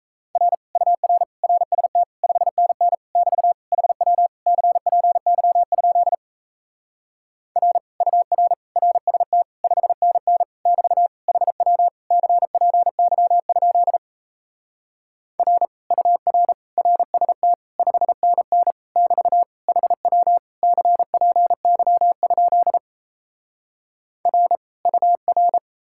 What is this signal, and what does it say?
Morse code, heard as a keyed tone, about 38 wpm, 700 Hz. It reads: R UR RST 5NN <BT> HW CPY? R UR RST 5NN <BT> HW CPY? R UR RST 5NN <BT> HW CPY? R UR